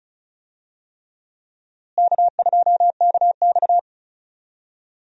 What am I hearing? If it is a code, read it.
Morse code, heard as a keyed tone, about 36 wpm, 700 Hz. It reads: K2KX
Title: USA Call Signs - 4 Characters - Repeated 3x - 35wpm